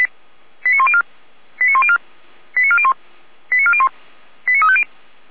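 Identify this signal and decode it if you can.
Five-tone selective call (ZVEI), heard as quick runs of five tones, tones of about 70 ms. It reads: 89104 89104 89401 89401 89350
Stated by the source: ZVEI five-tone call 89164 received on an SDR, recorded by Feuerwehr Orlamünde.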